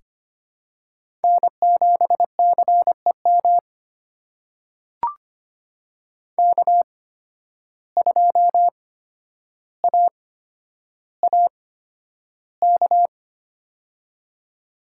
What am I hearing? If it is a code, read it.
Morse code, heard as a keyed tone, about 25 wpm, spaced as for 11 wpm, 700 Hz. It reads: N7CEM K2AAK
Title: USA Call Signs - 5 Characters - 8x Character Spacing - 25wpm